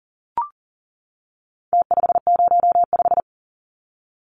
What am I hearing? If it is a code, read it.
Morse code, heard as a keyed tone, about 40 wpm, 700 Hz. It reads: T505
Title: Numbers - Contest Style - 4-Digit 40wpm